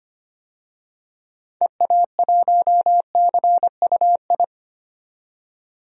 Morse code, heard as a keyed tone, about 25 wpm, 700 Hz. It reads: EA1CUI
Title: DXCC-100 Call Signs - 6 Characters - 25wpm